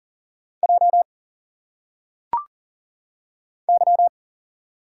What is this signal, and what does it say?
Morse code, heard as a keyed tone, about 40 wpm, 700 Hz. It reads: J Y
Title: Single Letters - Phonetic Pronunciation 40wpm